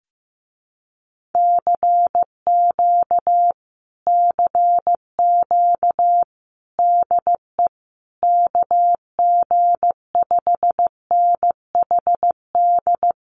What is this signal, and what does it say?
Morse code, heard as a keyed tone, about 15 wpm, 700 Hz. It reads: CQ CQ DE KG5NHD
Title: Common QSO Phrases Encore 15wpm